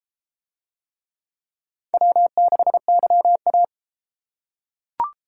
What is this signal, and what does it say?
Morse code, heard as a keyed tone, about 30 wpm, 700 Hz. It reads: W6YA